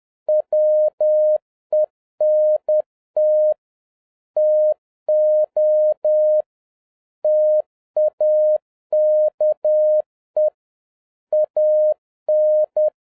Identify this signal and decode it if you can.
Morse code, heard as a keyed tone, about 10 wpm, 610 Hz. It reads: WENT TO TAKE AN